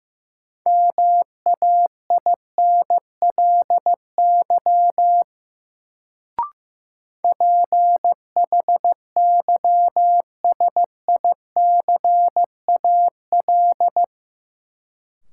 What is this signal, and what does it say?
Morse code, heard as a keyed tone, about 15 wpm, 700 Hz. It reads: MAINLY PHYSICAL
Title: Sentences from Top 1400 Words - New Word Drill Alt - 15wpm